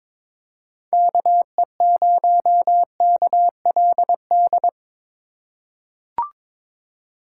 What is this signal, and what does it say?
Morse code, heard as a keyed tone, about 22 wpm, 700 Hz. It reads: KE0KLD
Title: USA Call Signs - 6 Characters - 2x Character Spacing - 22wpm